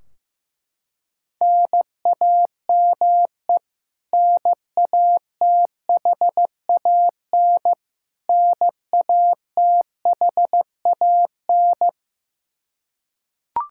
Morse code, heard as a keyed tone, about 16 wpm, 700 Hz. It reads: NAME NATHAN NATHAN
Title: Practice QSO - Exchange 1 - Beginning - 15wpm